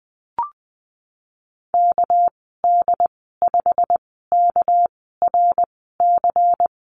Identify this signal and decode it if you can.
Morse code, heard as a keyed tone, about 19 wpm, spaced as for 14 wpm, 700 Hz. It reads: KD5KRC
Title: 200 Lesson 32g - Letter k - Call Signs 20wpm